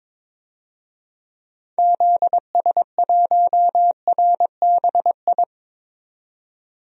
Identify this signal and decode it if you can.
Morse code, heard as a keyed tone, about 22 wpm, 700 Hz. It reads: ZS1RBI
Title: DXCC-100 Call Signs - 6 Characters - 22wpm